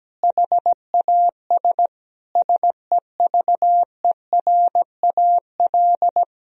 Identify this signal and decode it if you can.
Morse code, heard as a keyed tone, about 17 wpm, 700 Hz. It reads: HAS SEVERAL
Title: Sentences from Top 1000 Words - Review Part 2 - 17wpm